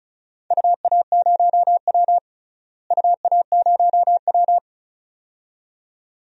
Morse code, heard as a keyed tone, about 34 wpm, 700 Hz. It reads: UA0W UA0W
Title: DXCC-100 Call Signs - 4 Characters - Repeated 2x - 35wpm